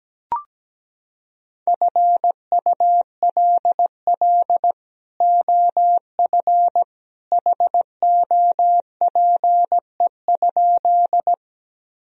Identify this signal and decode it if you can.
Morse code, heard as a keyed tone, about 17 wpm, 700 Hz. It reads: FULL OF HOPE?
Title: Sentences from Top 1000 Words - Review Part 3 - 17wpm